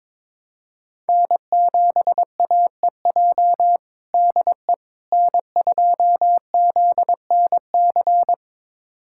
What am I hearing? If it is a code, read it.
Morse code, heard as a keyed tone, about 22 wpm, 700 Hz. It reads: N7AEJ DE N2ZNC
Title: USA Callsign DE Callsign 22wpm